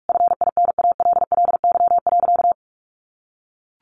Morse code, heard as a keyed tone, about 44 wpm, 700 Hz. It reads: FINALLY.
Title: Sentences from Top 1000 Words 45wpm